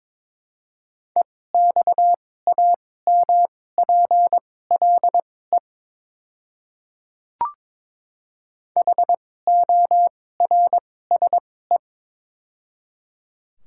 Morse code, heard as a keyed tone, about 22 wpm, spaced as for 16 wpm, 700 Hz. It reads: EXAMPLE HORSE
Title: 2x Character Spacing - Top 500 Words - 22wpm